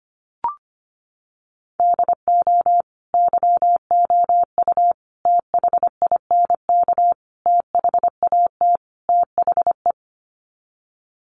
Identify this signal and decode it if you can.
Morse code, heard as a keyed tone, about 25 wpm, 700 Hz. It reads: DO YOU THINK THAT THE
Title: Sets of 5 Words Encore 25wpm